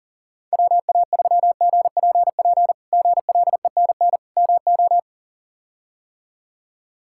Morse code, heard as a keyed tone, about 40 wpm, 700 Hz. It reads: WA3GPP GLENN MO
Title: K1USN SST QSO Practice - 40wpm